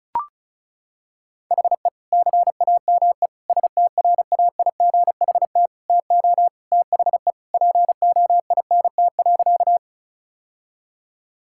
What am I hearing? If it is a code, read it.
Morse code, heard as a keyed tone, about 34 wpm, 700 Hz. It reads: HE CAME STRAIGHT TO THE POINT.
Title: Sentences from Top 1000 Words - Review Part 6 - 35wpm